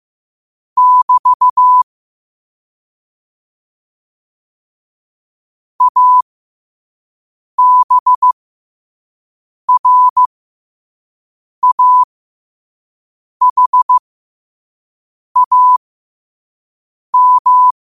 Morse code, heard as a keyed tone, about 15 wpm, spaced as for 5 wpm, 1000 Hz. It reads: <BT> ABRAHAM
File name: qotdfiles / 5wpm